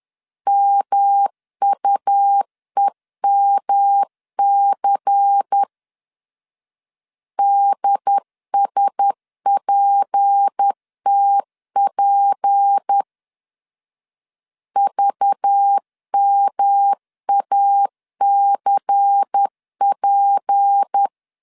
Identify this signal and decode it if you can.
Morse code, heard as a keyed tone, about 10 wpm, 800 Hz. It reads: MUEMC DSPTP VMACP
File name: LEZIONI / 6 PKX0 / 6 P 10wpm 2432